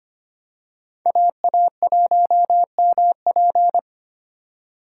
Morse code, heard as a keyed tone, about 25 wpm, 700 Hz. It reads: AA1MP
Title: USA Call Signs - 6x Character Spacing - 25wpm